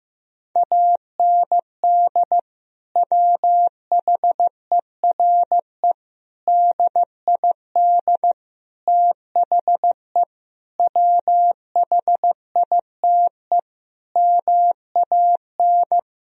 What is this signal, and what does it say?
Morse code, heard as a keyed tone, about 16 wpm, 700 Hz. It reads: AND WHERE DID THE WHITE MAN
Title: Sentences from Top 1000 Words - Review Part 1 - 15wpm